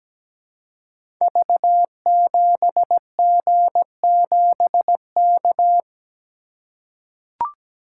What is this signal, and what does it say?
Morse code, heard as a keyed tone, about 17 wpm, 700 Hz. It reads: V7G7K
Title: Code Groups - 5 Letter:Numbers - 17wpm